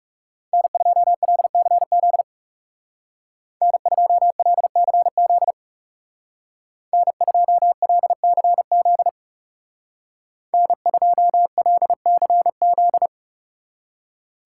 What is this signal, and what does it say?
Morse code, heard as a keyed tone, about 36 wpm, 700 Hz. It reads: N2LCZ N2LCZ N2LCZ N2LCZ
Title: Speed Racing - USA Call Signs 30wpm